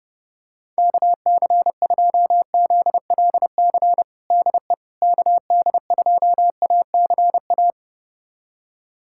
Morse code, heard as a keyed tone, about 32 wpm, 700 Hz. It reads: KC2ZLC DE KD2ACA
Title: USA Callsign DE Callsign 30wpm